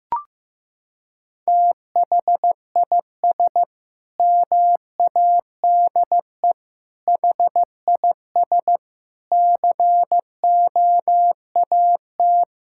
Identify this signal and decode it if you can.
Morse code, heard as a keyed tone, about 15 wpm, 700 Hz. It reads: THIS MADE HIS COAT
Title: Sentences from Top 1600 Words 15wpm